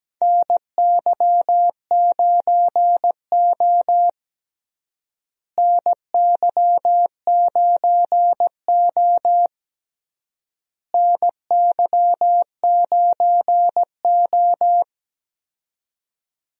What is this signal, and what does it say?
Morse code, heard as a keyed tone, about 17 wpm, 700 Hz. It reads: NY9O NY9O NY9O
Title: USA Call Signs - 4 Characters - Repeated 3x - 17wpm